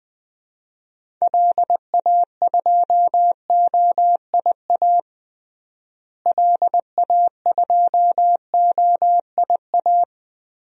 Morse code, heard as a keyed tone, about 20 wpm, 700 Hz. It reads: LA2OIA LA2OIA
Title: DXCC-100 Call Signs - 6 Characters - Repeated 2x - 20wpm